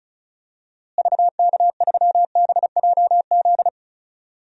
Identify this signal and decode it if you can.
Morse code, heard as a keyed tone, about 36 wpm, 700 Hz. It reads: VK3BJZ
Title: DXCC-100 Call Signs - 6 Characters - Repeated 3x - 35wpm